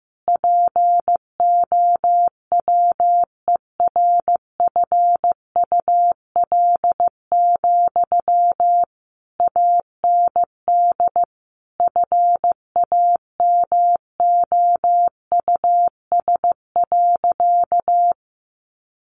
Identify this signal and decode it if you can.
Morse code, heard as a keyed tone, about 15 wpm, 700 Hz. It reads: POWERFUL, AND FAMOUS.
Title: Sentences from Top 1500 Words 15wpm